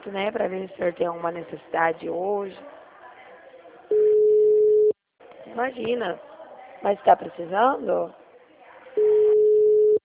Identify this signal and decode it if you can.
Ringback tone (Europe, 425 Hz), with speech.